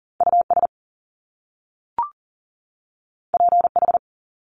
Morse code, heard as a keyed tone, about 41 wpm, 700 Hz. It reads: US PH
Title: Top Two Letter Combinations in Words 40wpm